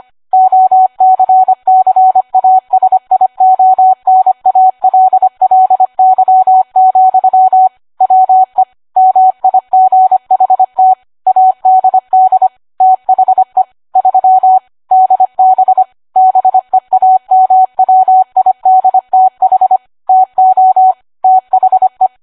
Morse code, heard as a keyed tone, about 25 wpm, 750 Hz. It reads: OCCASIONALLY, WE MIGHT ADD THE 3 DB BEAMWIDTH TO THE